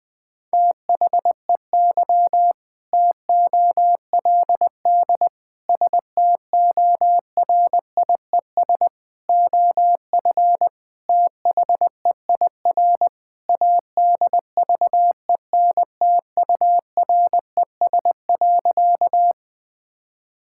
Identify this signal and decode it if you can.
Morse code, heard as a keyed tone, about 20 wpm, 700 Hz. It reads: THEY TOLD STORIES OF THEIR ADVENTURES.
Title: Sentences with 2-Word Phrases - 20wpm